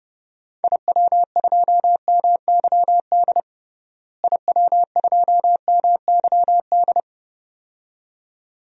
Morse code, heard as a keyed tone, about 30 wpm, 700 Hz. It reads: IW2MYD IW2MYD
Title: DXCC-100 Call Signs - 6 Characters - Repeated 2x - 30wpm